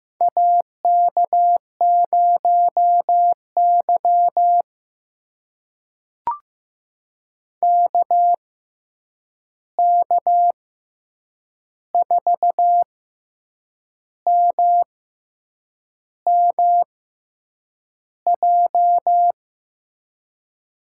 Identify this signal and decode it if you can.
Morse code, heard as a keyed tone, about 15 wpm, spaced as for 7 wpm, 700 Hz. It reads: AK0Y KK4MMJ
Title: USA Call Signs - 6x Character Spacing - 15wpm